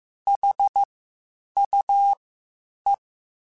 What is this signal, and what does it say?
Morse code, heard as a keyed tone, about 15 wpm, spaced as for 8 wpm, 800 Hz. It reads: HUE
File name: Misc / Misc-mp3s / WordPairs sp @13